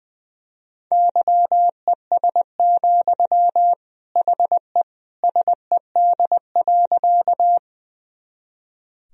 Morse code, heard as a keyed tone, about 20 wpm, 700 Hz. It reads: YES, HE SED.
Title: Sentences from Top 500 Words and Abbreviations 20wpm